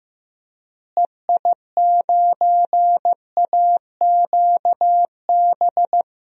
Morse code, heard as a keyed tone, about 15 wpm, 700 Hz. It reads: EI9AQB
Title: DXCC-100 Call Signs - 6 Characters - Repeated 2x - 15wpm